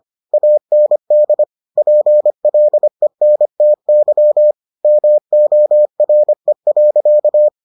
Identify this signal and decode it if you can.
Morse code, heard as a keyed tone, about 25 wpm, 600 Hz. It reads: AND PLENTY MORE.